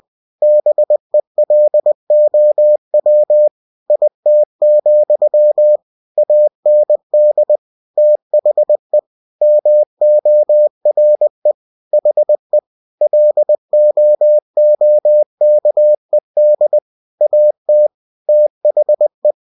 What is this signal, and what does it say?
Morse code, heard as a keyed tone, about 20 wpm, 600 Hz. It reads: BELOW IT, AND THE MORE HE LOOKED AT THE
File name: cw-books / winnie-the-pooh / winnie-20wpm-page-0031